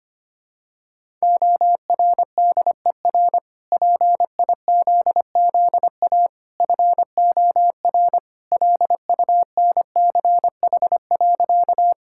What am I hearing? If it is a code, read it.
Morse code, heard as a keyed tone, about 25 wpm, 700 Hz. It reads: ORDER PIZZA FOR LUNCH.
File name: Easy Sentences with Hints - 25wpm